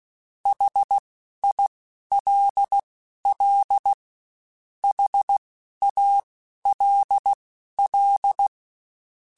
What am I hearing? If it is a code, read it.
Morse code, heard as a keyed tone, about 16 wpm, spaced as for 12 wpm, 800 Hz. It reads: HILL HALL
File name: H Prefixes / H mp3s